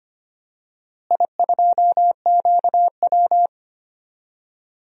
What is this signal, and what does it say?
Morse code, heard as a keyed tone, about 25 wpm, 700 Hz. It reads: I2QW
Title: DXCC-100 Call Signs - 4 Characters - Repeated 2x - 25wpm